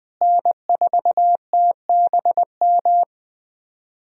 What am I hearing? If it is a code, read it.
Morse code, heard as a keyed tone, about 20 wpm, 700 Hz. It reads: N4TBM
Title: USA Call Signs - 5 Characters - Repeated 3x - 20wpm